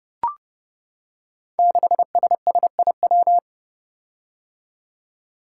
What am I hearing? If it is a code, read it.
Morse code, heard as a keyed tone, about 30 wpm, 700 Hz. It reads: 6SSIW
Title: Code Groups - 5 Letter:Numbers - 30wpm